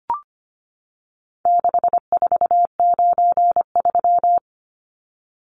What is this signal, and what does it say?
Morse code, heard as a keyed tone, about 25 wpm, 700 Hz. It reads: 6493
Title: Numbers - 4-Digit 25wpm